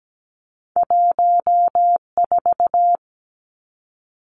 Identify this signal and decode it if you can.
Morse code, heard as a keyed tone, about 17 wpm, 700 Hz. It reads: 14
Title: Two Challenging Letter-Number Combinations 17wpm